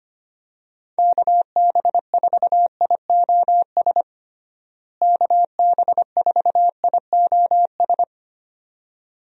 Morse code, heard as a keyed tone, about 25 wpm, 700 Hz. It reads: KB4IOS KB4IOS
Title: USA Call Signs - 6 Characters - Repeated 2x - 25wpm